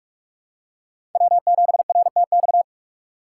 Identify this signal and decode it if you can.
Morse code, heard as a keyed tone, about 41 wpm, 700 Hz. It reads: W7RTX
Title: USA Call Signs - 3x Character Spacing - 45wpm